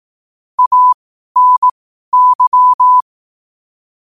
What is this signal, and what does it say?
Morse code, heard as a keyed tone, about 18 wpm, spaced as for 12 wpm, 1000 Hz. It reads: ANY
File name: qotdfiles / 12wpm